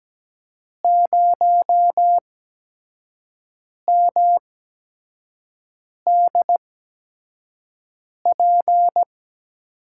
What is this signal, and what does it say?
Morse code, heard as a keyed tone, about 17 wpm, spaced as for 5 wpm, 700 Hz. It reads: 0MDP
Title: USA Call Signs - 5 Characters - 8x Character Spacing - 17wpm